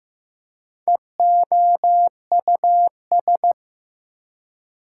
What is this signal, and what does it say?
Morse code, heard as a keyed tone, about 15 wpm, 700 Hz. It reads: EOUS
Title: Prefixes and Suffixes 15wpm